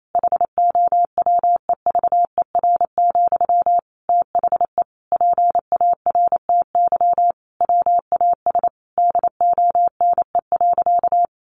Morse code, heard as a keyed tone, about 28 wpm, 700 Hz. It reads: HOWEVER, THE PARTY WAS DONE.
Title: Sentences from Top 1400 Words 28wpm